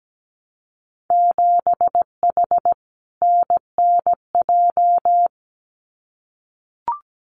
Morse code, heard as a keyed tone, about 17 wpm, 700 Hz. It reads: 7H NNJ
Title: Winter Field Day Exchanges Encore 17wpm